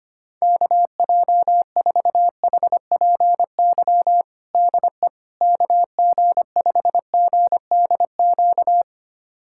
Morse code, heard as a keyed tone, about 25 wpm, 700 Hz. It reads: KJ4HPY DE KG5GDQ
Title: USA Callsign DE Callsign 25wpm